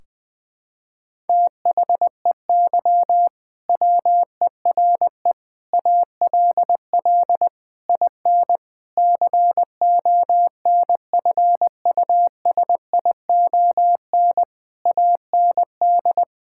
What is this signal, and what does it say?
Morse code, heard as a keyed tone, about 20 wpm, 700 Hz. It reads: THEY WERE ALL IN CONFUSION AND
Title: Sentences from Top 1600 Words 20wpm